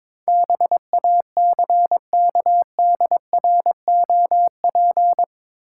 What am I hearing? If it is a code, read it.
Morse code, heard as a keyed tone, about 22 wpm, 700 Hz. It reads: BACKDROP